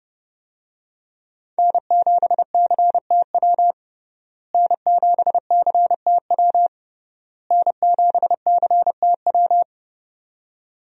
Morse code, heard as a keyed tone, about 30 wpm, 700 Hz. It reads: N7CTW N7CTW N7CTW
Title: USA Call Signs - 5 Characters - Repeated 3x - 30wpm